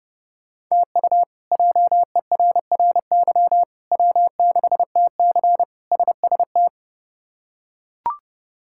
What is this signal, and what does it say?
Morse code, heard as a keyed tone, about 32 wpm, 700 Hz. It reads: TU JERRY W6TC SST
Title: K1USN SST QSO Practice - 30wpm